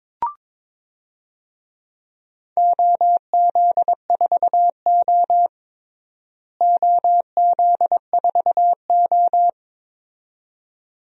Morse code, heard as a keyed tone, about 22 wpm, 700 Hz. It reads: OZ4O OZ4O
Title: DXCC-100 Call Signs - 4 Characters - Repeated 2x - 22wpm